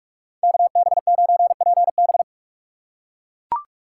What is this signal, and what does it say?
Morse code, heard as a keyed tone, about 44 wpm, 700 Hz. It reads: KB9PB